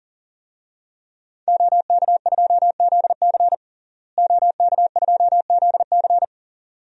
Morse code, heard as a keyed tone, about 40 wpm, 700 Hz. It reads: OK2ZC OK2ZC
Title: DXCC-100 Call Signs - 5 Characters - Repeated 2x - 40wpm